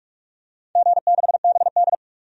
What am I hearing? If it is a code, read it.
Morse code, heard as a keyed tone, about 46 wpm, 700 Hz. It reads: G6BD
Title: DXCC-100 Call Signs - 4 Characters - 45wpm